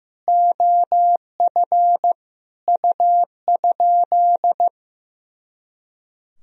Morse code, heard as a keyed tone, about 15 wpm, 700 Hz. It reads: OF U?
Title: Sentences from Top 300 Words and Abbreviations 15wpm